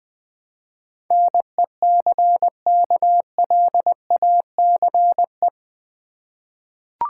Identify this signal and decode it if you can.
Morse code, heard as a keyed tone, about 20 wpm, 700 Hz. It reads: NECKLACE
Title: Easy Words with Hints - 20wpm